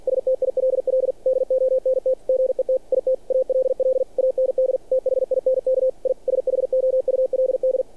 Morse code, heard as a keyed tone, about 48 wpm, 520 Hz. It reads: STILL DONT GET IT ALL AND THINK I SHOULD
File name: archived / CRSnet